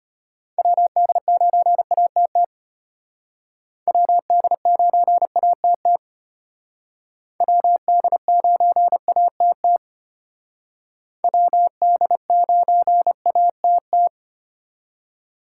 Morse code, heard as a keyed tone, about 32 wpm, 700 Hz. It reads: WD9ATT WD9ATT WD9ATT WD9ATT